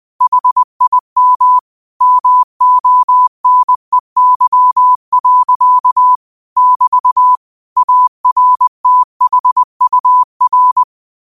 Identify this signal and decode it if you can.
Morse code, heard as a keyed tone, about 20 wpm, 1000 Hz. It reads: HIM MONEY. <BT> ARTHUR
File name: qotdfiles / 20wpm